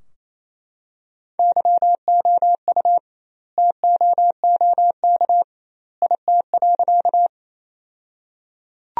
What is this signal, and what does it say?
Morse code, heard as a keyed tone, about 28 wpm, 700 Hz. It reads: YOU TOOK IT.
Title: 2x Word-Spacing - Sentences from Top 300 Words - 28wpm